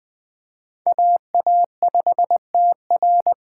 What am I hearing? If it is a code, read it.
Morse code, heard as a keyed tone, about 20 wpm, 700 Hz. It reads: AA5TR
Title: USA Call Signs - 5 Characters - 3x Character Spacing - 20wpm